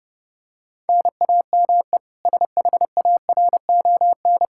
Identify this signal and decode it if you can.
Morse code, heard as a keyed tone, about 32 wpm, 700 Hz. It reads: NAME SHARON
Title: K1USN SST QSO Practice 30wpm